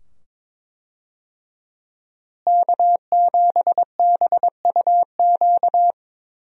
Morse code, heard as a keyed tone, about 22 wpm, 700 Hz. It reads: K7BUQ
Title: USA Call Signs - 5 Characters - Repeated 2x - 22wpm